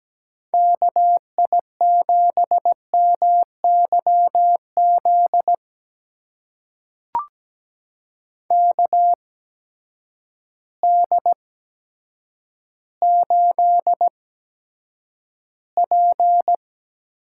Morse code, heard as a keyed tone, about 17 wpm, spaced as for 9 wpm, 700 Hz. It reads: KI7MYZ KD8P
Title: USA Call Signs - 8x Character Spacing - 17wpm